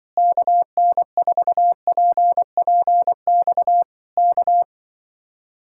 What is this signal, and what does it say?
Morse code, heard as a keyed tone, about 23 wpm, 700 Hz. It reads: KN4PPX K
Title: Speed-Racing Common QSO Phrases Encore - 20wpm